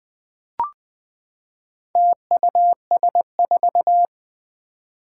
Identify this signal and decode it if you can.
Morse code, heard as a keyed tone, about 20 wpm, 700 Hz. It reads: TUS4
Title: Four Challenging Letter-Number Combinations 20wpm